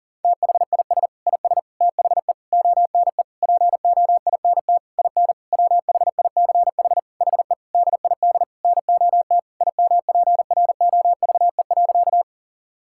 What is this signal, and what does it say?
Morse code, heard as a keyed tone, about 40 wpm, 700 Hz. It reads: THIS IS THE ONE POINT IN WHICH HE DID NOT IMPROVE.
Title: Sentences from Top 1000 Words - Review Part 5 - 40wpm